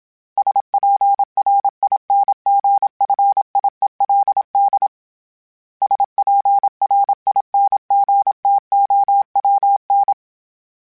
Morse code, heard as a keyed tone, about 27 wpm, 800 Hz. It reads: SPRINGFIELD SPRINGTOWN
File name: fromSMs / FromSMs mp3s / Cities@25